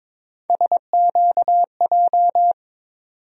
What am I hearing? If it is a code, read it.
Morse code, heard as a keyed tone, about 21 wpm, 700 Hz. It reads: SQJ
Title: USA Call Sign Postfixes - 3 Letters 22wpm